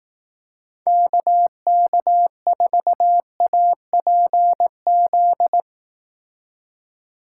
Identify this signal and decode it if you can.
Morse code, heard as a keyed tone, about 18 wpm, 700 Hz. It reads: KK4APZ
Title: Speed Racing - USA 2x3 Call Signs 15wpm